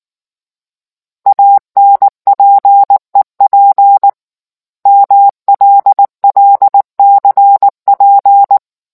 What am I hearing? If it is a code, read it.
Morse code, heard as a keyed tone, about 19 wpm, 800 Hz. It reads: ANPEP MLLCP